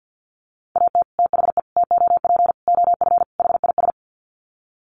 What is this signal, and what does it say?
Morse code, heard as a keyed tone, about 50 wpm, 700 Hz. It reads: AT THE TOP OF HIS
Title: Sets of 5 Words Encore 50wpm